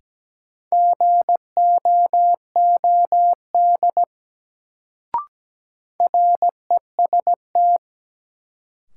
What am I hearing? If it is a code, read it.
Morse code, heard as a keyed tone, about 17 wpm, 700 Hz. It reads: GOOD REST
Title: Sets of 2 Words - New Word Drill - 17wpm